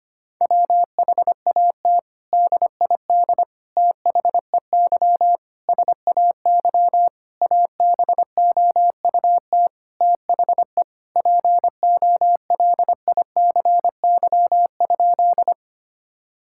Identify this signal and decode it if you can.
Morse code, heard as a keyed tone, about 25 wpm, 700 Hz. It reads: WHAT DID THEY SAY ABOUT THE POLICY?
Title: Sentences from Top 1000 Words - Review Part 5 - 25wpm